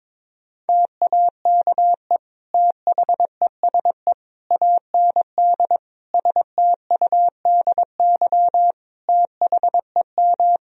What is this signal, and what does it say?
Morse code, heard as a keyed tone, about 22 wpm, 700 Hz. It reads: TAKE THESE AND STUDY THEM
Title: Sentences from Top 1000 Words - Review Part 1 - 22wpm